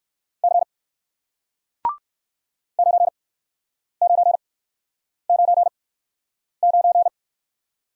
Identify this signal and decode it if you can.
Morse code, heard as a keyed tone, about 57 wpm, 700 Hz. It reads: C 9 9 9 9